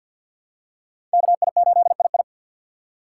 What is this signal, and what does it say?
Morse code, heard as a keyed tone, about 47 wpm, 700 Hz. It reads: KI8II
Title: USA Call Signs - 4x Character Spacing - 50wpm